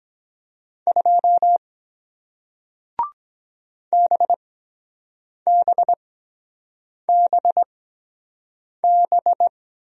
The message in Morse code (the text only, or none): 2 BBBB